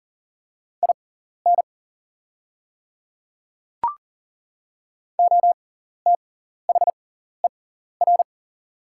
Morse code, heard as a keyed tone, about 40 wpm, spaced as for 14 wpm, 690 Hz. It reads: IN OTHER
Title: 6x Character Spacing - Top 100 Words - 40wpm